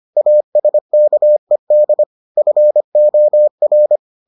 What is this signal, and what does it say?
Morse code, heard as a keyed tone, about 25 wpm, 600 Hz. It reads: ASKED FOR